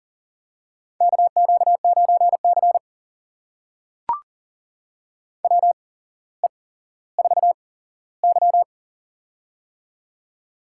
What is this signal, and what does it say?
Morse code, heard as a keyed tone, about 42 wpm, spaced as for 17 wpm, 700 Hz. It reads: KQ9C WE4Y
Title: USA Call Signs - 4 Characters - 8x Character Spacing - 40wpm